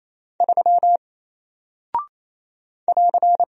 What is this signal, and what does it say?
Morse code, heard as a keyed tone, about 27 wpm, 700 Hz. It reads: 3 <AR>
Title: Single Character (Letter:Number:Punctuation:Prosign) Alt 28wpm